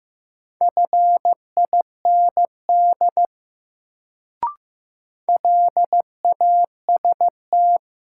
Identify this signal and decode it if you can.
Morse code, heard as a keyed tone, about 15 wpm, 700 Hz. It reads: FIND LAST
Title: Sets of 2 Words - New Word Drill - 15wpm